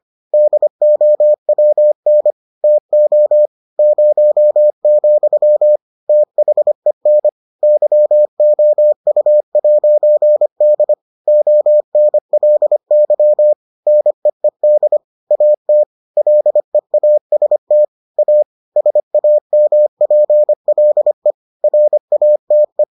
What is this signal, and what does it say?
Morse code, heard as a keyed tone, about 25 wpm, 600 Hz. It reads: DOWN TO 0, THEN YOU'D ONLY NEED AT LEAST A SAMPLE RATE